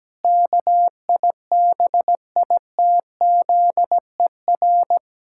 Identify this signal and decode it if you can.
Morse code, heard as a keyed tone, about 17 wpm, 700 Hz. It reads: KIBITZER
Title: Pangram Challenge v2 - 17wpm